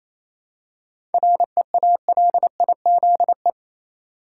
Morse code, heard as a keyed tone, about 28 wpm, 700 Hz. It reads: REALIZE
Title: Seven Letter Words - 28wpm